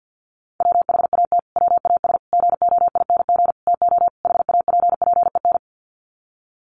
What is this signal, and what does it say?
Morse code, heard as a keyed tone, about 49 wpm, 700 Hz. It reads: WHAT WAS GOING TO HAPPEN
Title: Sets of 5 Words Encore 50wpm